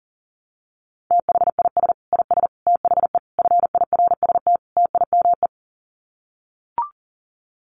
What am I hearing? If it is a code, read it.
Morse code, heard as a keyed tone, about 40 wpm, 700 Hz. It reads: THIS IS THE FIRST TIME
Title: Sets of 5 Words 40wpm